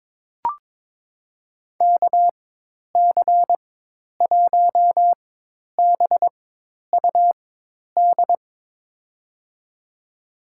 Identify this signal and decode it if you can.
Morse code, heard as a keyed tone, about 22 wpm, spaced as for 10 wpm, 700 Hz. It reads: KC1BUD
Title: USA Call Signs - 4x Character Spacing - 22wpm